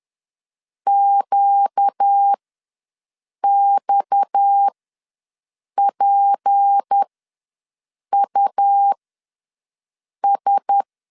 Morse code, heard as a keyed tone, about 10 wpm, spaced as for 6 wpm, 800 Hz. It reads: QXPUS
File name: LEZIONI / 6 PKX0 / 6 PKX0 ws 10wpm 4220